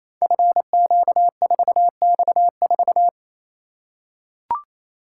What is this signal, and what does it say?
Morse code, heard as a keyed tone, about 28 wpm, 700 Hz. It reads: FQ4X4
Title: Code Groups - 5 Letter:Numbers - 28wpm